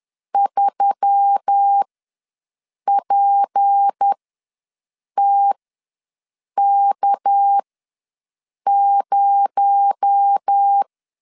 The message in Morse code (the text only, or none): 3PTK0